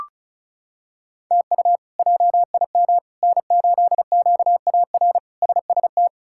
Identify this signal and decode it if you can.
Morse code, heard as a keyed tone, about 34 wpm, 700 Hz. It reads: TU JIM N8QAR SST
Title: K1USN SST QSO Practice - 35wpm